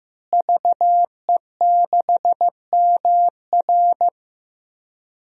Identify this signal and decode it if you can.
Morse code, heard as a keyed tone, about 15 wpm, 700 Hz. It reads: VE6MR
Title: DXCC-100 Call Signs - 5 Characters - Repeated 3x - 15wpm